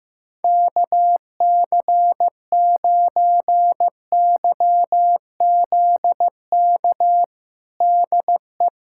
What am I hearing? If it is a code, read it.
Morse code, heard as a keyed tone, about 15 wpm, 700 Hz. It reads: KC9YZK DE
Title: USA Callsign DE Callsign 15wpm